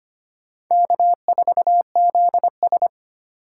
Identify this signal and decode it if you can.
Morse code, heard as a keyed tone, about 32 wpm, 700 Hz. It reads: K4ZS